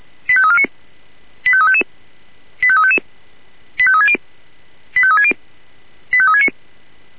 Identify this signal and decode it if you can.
Five-tone selective call (ZVEI), heard as quick runs of five tones, tones of about 70 ms. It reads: 85349 85355 85360 85360 85369